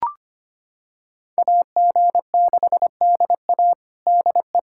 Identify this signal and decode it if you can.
Morse code, heard as a keyed tone, about 25 wpm, 700 Hz. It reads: AG6DA DE
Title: USA Callsign DE Callsign - 5 Characters - 25wpm